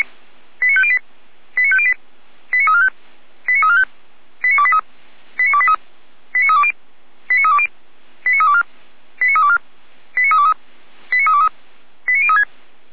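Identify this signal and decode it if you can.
Five-tone selective call (ZVEI), heard as quick runs of five tones, tones of about 70 ms. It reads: 89508 89508 89345 89345 89283 89283 89320 89320 89324 89324 89323 89323 89036
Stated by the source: ZVEI five-tone call received by SDR.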